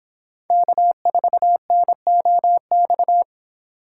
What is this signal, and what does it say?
Morse code, heard as a keyed tone, about 24 wpm, 700 Hz. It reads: K4NOX